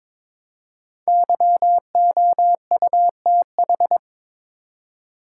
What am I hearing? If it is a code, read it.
Morse code, heard as a keyed tone, about 22 wpm, 700 Hz. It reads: YOUTH
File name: 5 Letter Words - First 3 Letters Given - 22wpm